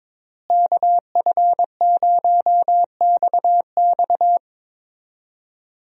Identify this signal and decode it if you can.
Morse code, heard as a keyed tone, about 22 wpm, 700 Hz. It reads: KF0XX
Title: USA Call Signs - 5 Characters - Repeated 2x - 22wpm